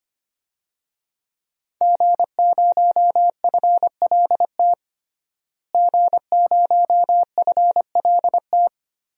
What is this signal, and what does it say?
Morse code, heard as a keyed tone, about 25 wpm, 700 Hz. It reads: G0FLT G0FLT
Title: DXCC-100 Call Signs - 5 Characters - Repeated 2x - 25wpm